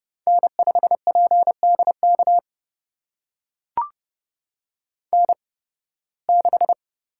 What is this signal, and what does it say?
Morse code, heard as a keyed tone, about 30 wpm, spaced as for 9 wpm, 700 Hz. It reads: N5PD KN6
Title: USA Call Signs - 8x Character Spacing - 30wpm